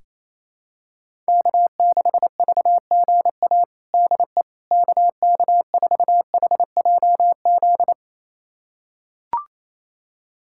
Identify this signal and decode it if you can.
Morse code, heard as a keyed tone, about 28 wpm, 700 Hz. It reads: K6VGA DE KK4HJZ
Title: USA Callsign DE Callsign 28wpm